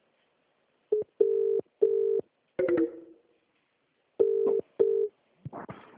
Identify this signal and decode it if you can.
Ringback tone (UK), 400 + 450 Hz.